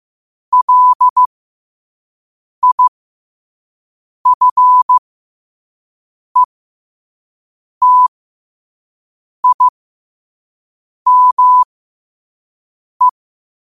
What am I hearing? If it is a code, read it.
Morse code, heard as a keyed tone, about 15 wpm, spaced as for 5 wpm, 1000 Hz. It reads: LIFETIME
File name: qotdfiles / 5wpm